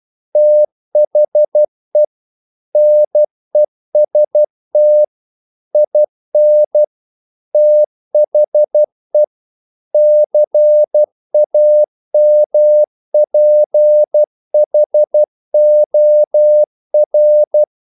Morse code, heard as a keyed tone, about 12 wpm, 600 Hz. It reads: THE NEST IN THE CAMPHOR
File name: mp3 files / the gardener - 12wpm